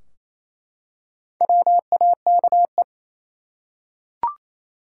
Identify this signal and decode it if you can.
Morse code, heard as a keyed tone, about 28 wpm, 700 Hz. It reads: WAKE
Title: Four Letter Words - 28wpm